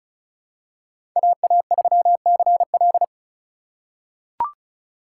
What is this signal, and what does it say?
Morse code, heard as a keyed tone, about 36 wpm, 700 Hz. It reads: AA3CL
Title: USA Call Signs - 3x Character Spacing - 35wpm